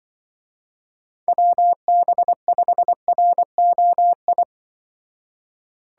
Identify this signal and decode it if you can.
Morse code, heard as a keyed tone, about 24 wpm, 700 Hz. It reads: WB5ROI
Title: Speed Racing - USA 2x3 Call Signs 20wpm